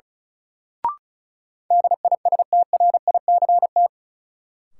Morse code, heard as a keyed tone, about 36 wpm, 700 Hz. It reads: DISTRICT
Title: Sentences from Top 1600 Words - New Word Drill Alt - 35wpm